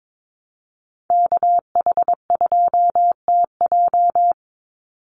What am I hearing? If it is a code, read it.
Morse code, heard as a keyed tone, about 22 wpm, 700 Hz. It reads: KH2TJ
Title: NA SOTA Activator Callsigns 22wpm